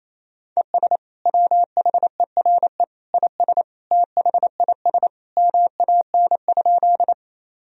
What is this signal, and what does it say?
Morse code, heard as a keyed tone, about 27 wpm, 700 Hz. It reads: ES WHERE IS THIS MAN?
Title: Sentences from Top 200 Words and Abbreviations 28wpm